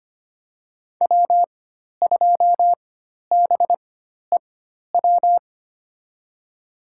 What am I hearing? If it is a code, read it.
Morse code, heard as a keyed tone, about 25 wpm, spaced as for 12 wpm, 700 Hz. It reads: W2BEW
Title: USA Call Signs - 5 Characters - 4x Character Spacing - 25wpm